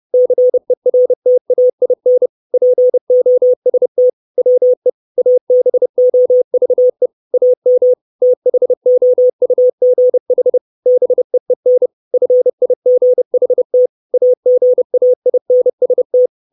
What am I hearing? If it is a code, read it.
Morse code, heard as a keyed tone, about 30 wpm, 500 Hz. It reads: CERTAIN POST WE ABOVE AM THOUGH BEEN FIGHT AGAINST
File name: 632 Words 30WPM